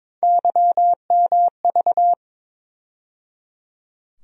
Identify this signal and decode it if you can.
Morse code, heard as a keyed tone, about 21 wpm, 700 Hz. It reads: YMV
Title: USA Call Sign Postfixes - 3 Letters 22wpm